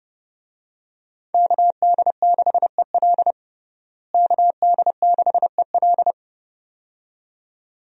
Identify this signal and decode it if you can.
Morse code, heard as a keyed tone, about 30 wpm, 700 Hz. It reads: KD6EL KD6EL
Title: USA Call Signs - 5 Characters - Repeated 2x - 30wpm